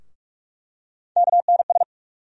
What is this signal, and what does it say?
Morse code, heard as a keyed tone, about 42 wpm, 700 Hz. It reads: KNS
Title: USA Call Sign Postfixes - 3 Letters 45wpm